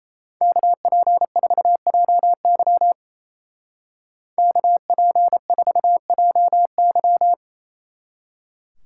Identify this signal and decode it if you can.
Morse code, heard as a keyed tone, about 34 wpm, 700 Hz. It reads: KP4JY KP4JY
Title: Speed Racing - USA Call Signs 28wpm